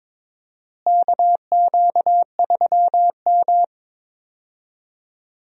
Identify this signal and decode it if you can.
Morse code, heard as a keyed tone, about 22 wpm, 700 Hz. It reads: KQ3M
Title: USA Call Signs - 4 Characters - Repeated 3x - 22wpm